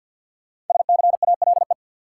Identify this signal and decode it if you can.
Morse code, heard as a keyed tone, about 53 wpm, 700 Hz. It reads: SCALE
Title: Sentences from Top 1000 Words - Review Part 4 - New Word Drill - 50wpm